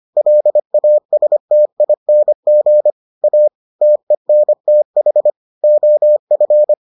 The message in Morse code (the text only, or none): LASTING A TENTH OF